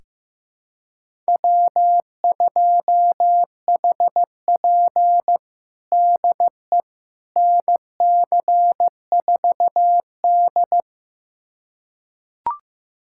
Morse code, heard as a keyed tone, about 15 wpm, 700 Hz. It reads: W2HP DE NC4D
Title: USA Callsign DE Callsign - 4 Characters - 15wpm